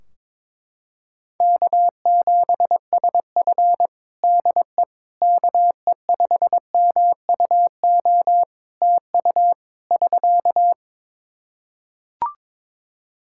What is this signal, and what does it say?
Morse code, heard as a keyed tone, about 22 wpm, 700 Hz. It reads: K7SF DE KE5MUO TU <SK>
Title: K1USN SST QSO Practice 22wpm